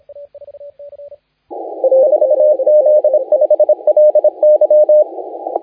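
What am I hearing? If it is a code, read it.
Morse code, heard as a keyed tone, about 26 wpm, 600 Hz, with radio noise. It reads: 4Z5LY
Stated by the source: Morse contest audio recorded by LA4C during the SAC CW 2018 contest.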